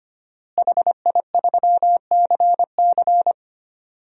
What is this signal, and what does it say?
Morse code, heard as a keyed tone, about 25 wpm, 700 Hz. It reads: HI3CC
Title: DXCC-100 Call Signs - 5 Characters - Repeated 3x - 25wpm